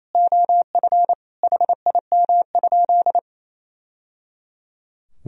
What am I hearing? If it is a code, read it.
Morse code, heard as a keyed tone, about 28 wpm, 700 Hz. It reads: OF HIM?
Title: Sentences from Top 1000 Words - Review Part 4 - 28wpm